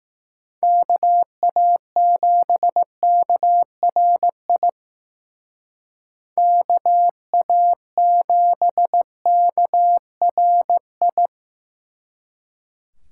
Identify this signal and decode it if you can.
Morse code, heard as a keyed tone, about 19 wpm, 700 Hz. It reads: KA7KRI KA7KRI
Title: Speed Racing - USA Call Signs 15wpm